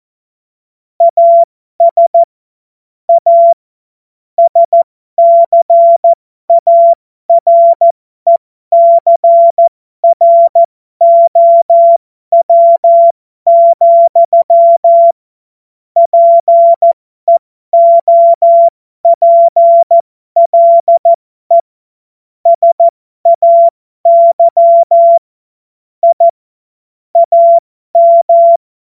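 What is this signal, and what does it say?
Morse code, heard as a keyed tone, about 14 wpm, spaced as for 12 wpm, 680 Hz. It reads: AS A SCARECROW, PEOPLE SAY I AM
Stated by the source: Morse code joke.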